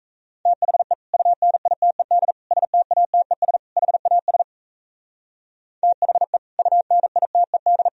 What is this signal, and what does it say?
Morse code, heard as a keyed tone, about 32 wpm, 700 Hz. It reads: THE UNITED STATES HAS THE UNITED
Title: Speed-Racing - Sets of 4 Words 28wpm